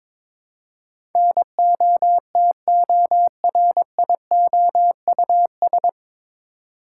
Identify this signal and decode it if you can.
Morse code, heard as a keyed tone, about 23 wpm, 700 Hz. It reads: NOTORIOUS
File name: 9 Letter Words - First 3 Letters Given - 22wpm